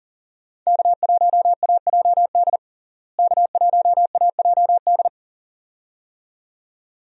Morse code, heard as a keyed tone, about 40 wpm, 700 Hz. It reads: K1AJD K1AJD
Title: USA Call Signs - 5 Characters - Repeated 2x - 40wpm